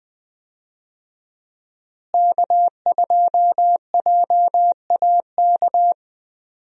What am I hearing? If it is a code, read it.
Morse code, heard as a keyed tone, about 20 wpm, 700 Hz. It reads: K2JAK
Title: USA Call Signs - 5 Characters - Repeated 2x - 20wpm